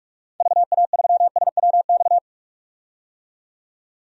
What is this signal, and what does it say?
Morse code, heard as a keyed tone, about 42 wpm, 700 Hz. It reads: VA3SWX
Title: DXCC-100 Call Signs - 6 Characters - 45wpm